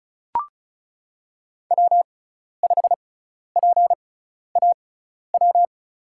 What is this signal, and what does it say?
Morse code, heard as a keyed tone, about 34 wpm, spaced as for 12 wpm, 700 Hz. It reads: W5PAW
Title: USA Call Signs - 5 Characters - 6x Character Spacing - 35wpm